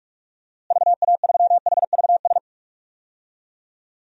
Morse code, heard as a keyed tone, about 46 wpm, 700 Hz. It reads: VA3HVS
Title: DXCC-100 Call Signs - 6 Characters - Repeated 2x - 45wpm